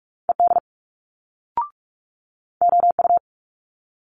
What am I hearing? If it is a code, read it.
Morse code, heard as a keyed tone, about 46 wpm, 700 Hz. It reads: ED OU
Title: Multi-letter Phonemes Quick Fire 45wpm